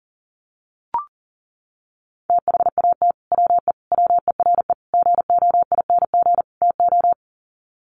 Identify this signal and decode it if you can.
Morse code, heard as a keyed tone, about 40 wpm, 700 Hz. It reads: THAT WE WERE GOING TO
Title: Sets of 5 Words Encore2 40wpm